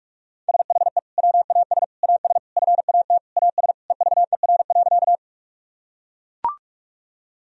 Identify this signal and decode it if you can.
Morse code, heard as a keyed tone, about 41 wpm, 700 Hz. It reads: SHE WAS AS FAT AS EVER.
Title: Sentences from Top 1000 Words - Review Part 4 - 45wpm